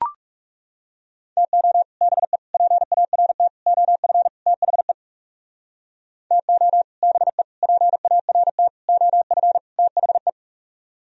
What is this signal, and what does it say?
Morse code, heard as a keyed tone, about 37 wpm, 700 Hz. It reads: TO BE PART OF THE TO BE PART OF THE
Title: Speed-Racing - Sets of 5 Words 30wpm